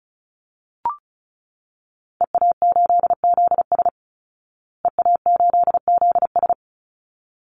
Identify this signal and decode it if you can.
Morse code, heard as a keyed tone, about 34 wpm, 700 Hz. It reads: EA8ZS EA8ZS
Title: Call Signs Repeated 3x - 35wpm